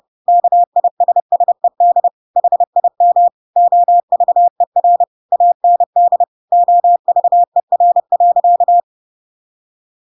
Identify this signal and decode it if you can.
Morse code, heard as a keyed tone, about 30 wpm, 700 Hz. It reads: KISSED HIM OVER AND OVER.
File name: Pinocchio / Pinocchio ch8 30 30 ik1hge